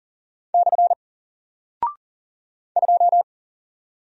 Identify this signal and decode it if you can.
Morse code, heard as a keyed tone, about 40 wpm, 700 Hz. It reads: / 2
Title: Single Character (Letter:Number:Punctuation:Prosign) Alt 40wpm